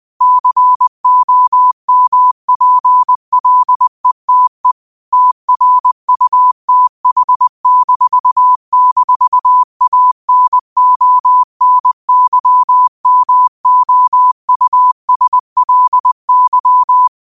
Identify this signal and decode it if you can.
Morse code, heard as a keyed tone, about 20 wpm, 1000 Hz. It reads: COMPLETE TRUTH--ANONYMOUSLY
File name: qotdfiles / 20wpm